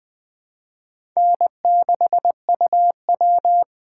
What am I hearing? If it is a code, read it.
Morse code, heard as a keyed tone, about 20 wpm, 700 Hz. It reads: N6UW
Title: USA Call Signs - 4 Characters - 2x Character Spacing - 20wpm